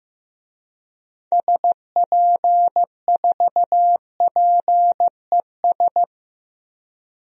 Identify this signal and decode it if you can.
Morse code, heard as a keyed tone, about 16 wpm, 700 Hz. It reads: SP4PES
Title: DXCC-100 Call Signs - 6 Characters - Repeated 2x - 15wpm